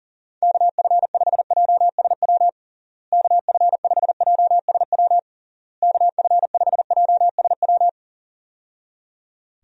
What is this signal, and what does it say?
Morse code, heard as a keyed tone, about 40 wpm, 700 Hz. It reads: KF5JSW KF5JSW KF5JSW
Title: USA Call Signs - 6 Characters - Repeated 3x - 40wpm